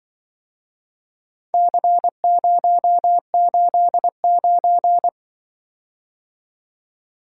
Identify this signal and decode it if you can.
Morse code, heard as a keyed tone, about 24 wpm, 700 Hz. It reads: C089
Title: Speed-Racing - Four Letter-Number Combinations - 20wpm